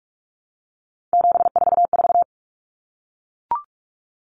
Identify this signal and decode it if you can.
Morse code, heard as a keyed tone, about 44 wpm, 700 Hz. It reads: 744
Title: Numbers - 3-Digit 45wpm